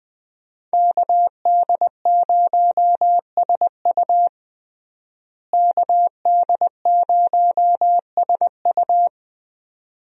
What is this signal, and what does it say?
Morse code, heard as a keyed tone, about 20 wpm, 700 Hz. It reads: KD0SU KD0SU
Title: USA Call Signs - 5 Characters - Repeated 2x - 20wpm